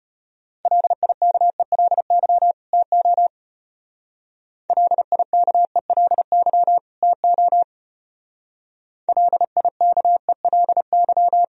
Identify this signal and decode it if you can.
Morse code, heard as a keyed tone, about 31 wpm, 700 Hz. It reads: LIKELY TO LIKELY TO LIKELY
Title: Speed-Racing - Sets of 2 Words 25wpm